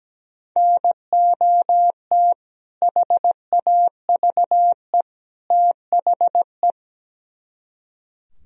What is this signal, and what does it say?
Morse code, heard as a keyed tone, about 17 wpm, 700 Hz. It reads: NOT HAVE THE
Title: Speed-Racing - Sets of 3 Words 17wpm